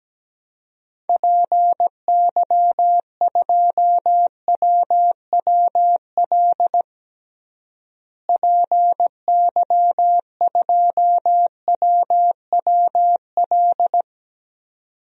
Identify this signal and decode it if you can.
Morse code, heard as a keyed tone, about 17 wpm, 700 Hz. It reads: PY2WWL PY2WWL
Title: DXCC-100 Call Signs - 6 Characters - Repeated 2x - 17wpm